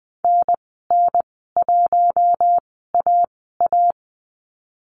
Morse code, heard as a keyed tone, about 20 wpm, spaced as for 15 wpm, 700 Hz. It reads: NN1AA
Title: 088 Lesson 15g - Number 5 - Call Signs 20wpm